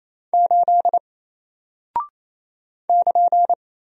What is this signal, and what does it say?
Morse code, heard as a keyed tone, about 27 wpm, 700 Hz. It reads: 8 <KN>
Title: Single Character (Letter:Number:Punctuation:Prosign) Alt 28wpm